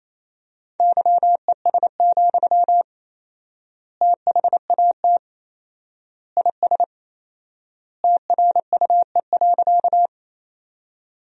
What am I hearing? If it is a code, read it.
Morse code, heard as a keyed tone, about 28 wpm, 700 Hz. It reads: YES, THAT IS TRUE.
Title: 4x Word-Spacing - Sentences from Top 500 Words - 28wpm